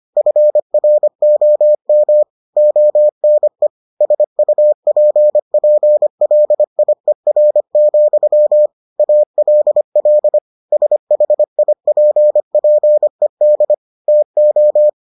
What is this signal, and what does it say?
Morse code, heard as a keyed tone, about 25 wpm, 600 Hz. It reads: FROM ONE SUPPLIER, ALL SHIPPED TO